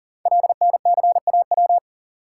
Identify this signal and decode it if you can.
Morse code, heard as a keyed tone, about 40 wpm, 700 Hz. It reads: LNCAW
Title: Code Groups - 5 Letters - 40wpm